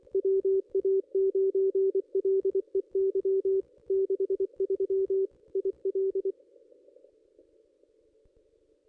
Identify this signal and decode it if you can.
Morse code, heard as a keyed tone, about 24 wpm, 390 Hz. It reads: WA9LEY 63 IL
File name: contests / ss-cw-k5tr-op-n5rz -1800